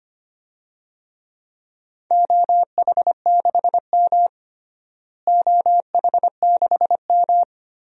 Morse code, heard as a keyed tone, about 25 wpm, 700 Hz. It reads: OH6M OH6M
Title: DXCC-100 Call Signs - 4 Characters - Repeated 3x - 25wpm